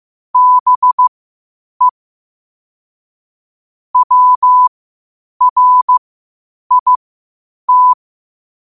Morse code, heard as a keyed tone, about 15 wpm, spaced as for 8 wpm, 1000 Hz. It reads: BE WRIT
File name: qotdfiles / 8wpm